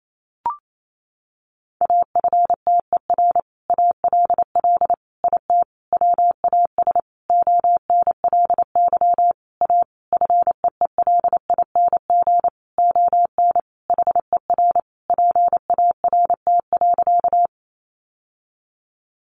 Morse code, heard as a keyed tone, about 28 wpm, 700 Hz. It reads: AFTER ALL IT WAS ONLY A FEELING ON HER PART.
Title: Sentences from Top 1200 Words 28wpm